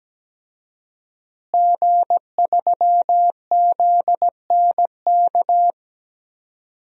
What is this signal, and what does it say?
Morse code, heard as a keyed tone, about 17 wpm, 700 Hz. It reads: G3ZNK
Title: DXCC-100 Call Signs - 5 Characters - Repeated 3x - 17wpm